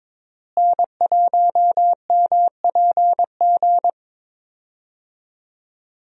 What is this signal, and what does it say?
Morse code, heard as a keyed tone, about 21 wpm, 700 Hz. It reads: N1MPG
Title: USA Call Signs - 5 Characters - Repeated 2x - 22wpm